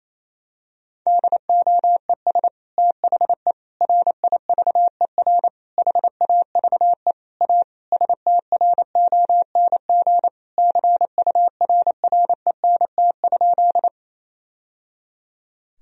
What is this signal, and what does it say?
Morse code, heard as a keyed tone, about 28 wpm, 700 Hz. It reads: DOES THE RIVER HAVE A STRONG CURRENT?
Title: Sentences from Top 1000 Words - Review Part 4 - 28wpm